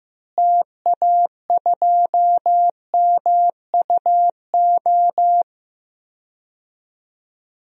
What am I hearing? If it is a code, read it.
Morse code, heard as a keyed tone, about 16 wpm, 700 Hz. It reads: TA2MUO
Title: DXCC-100 Call Signs - 6 Characters - Repeated 3x - 15wpm